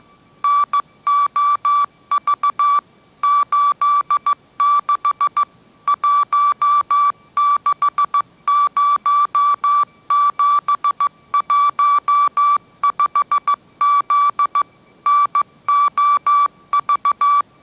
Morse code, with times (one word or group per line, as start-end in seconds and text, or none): 0.4-2.8 NOV
3.2-5.4 86
5.9-14.6 160715Z
15.1-17.4 NOV